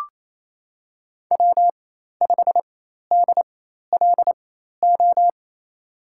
Morse code, heard as a keyed tone, about 28 wpm, spaced as for 13 wpm, 690 Hz. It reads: W5DLO
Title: USA Call Signs - 5 Characters - 4x Character Spacing - 28wpm